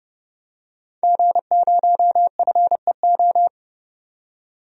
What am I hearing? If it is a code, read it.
Morse code, heard as a keyed tone, about 30 wpm, 690 Hz. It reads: G0FEO
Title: DXCC-100 Call Signs - 5 Characters - 30wpm